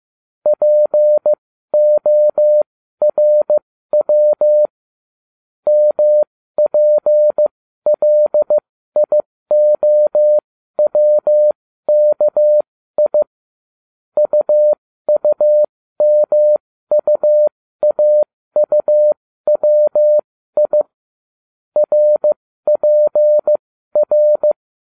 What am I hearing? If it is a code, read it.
Morse code, heard as a keyed tone, about 15 wpm, spaced as for 12 wpm, 600 Hz. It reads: PORW MPLIOWKI UUMUAUWI RPR